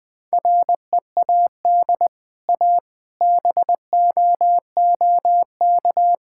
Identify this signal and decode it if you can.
Morse code, heard as a keyed tone, about 20 wpm, 700 Hz. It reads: READ A BOOK
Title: Easy Sentences with Hints - 20wpm